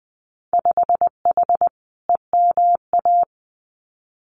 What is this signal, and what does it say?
Morse code, heard as a keyed tone, about 20 wpm, 700 Hz. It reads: 5H EMA
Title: Winter Field Day Exchanges Encore 20wpm